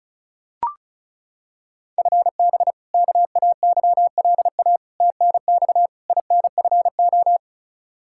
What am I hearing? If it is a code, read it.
Morse code, heard as a keyed tone, about 36 wpm, 700 Hz. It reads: FB KAYLA TNX INFO
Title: K1USN SST QSO Practice 35wpm